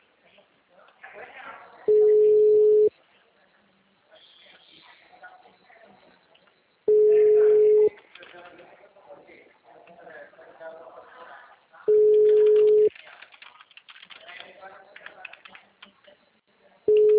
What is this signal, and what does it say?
Ringback tone (Europe, 425 Hz), with speech.
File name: out- -2025- - - .34277